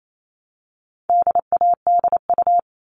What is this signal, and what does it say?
Morse code, heard as a keyed tone, about 28 wpm, 700 Hz. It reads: DADU
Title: All Cut Numbers - 4-Digit 28wpm